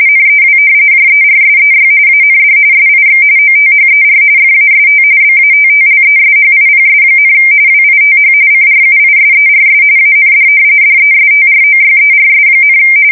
RTTY, about 45.45 baud. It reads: NEW JERSEY AND SIX OTHER FAST BATTLESHIPS, WITH CRUISERS AND DESTROYERS,
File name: fsk170 / battleshipnj